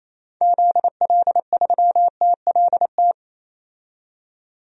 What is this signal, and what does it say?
Morse code, heard as a keyed tone, about 27 wpm, 700 Hz. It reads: ZL3TLT
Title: DXCC-100 Call Signs - 6 Characters - 28wpm